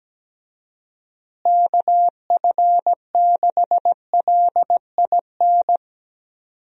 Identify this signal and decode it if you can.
Morse code, heard as a keyed tone, about 17 wpm, 700 Hz. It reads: KF6LIN
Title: USA Call Signs - 6 Characters - Repeated 2x - 17wpm